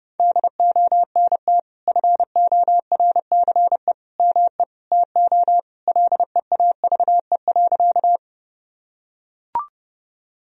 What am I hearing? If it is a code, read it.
Morse code, heard as a keyed tone, about 32 wpm, 700 Hz. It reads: DONT FORCE ME TO LEAVE.
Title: Sentences from Top 1000 Words - Review Part 3 - 30wpm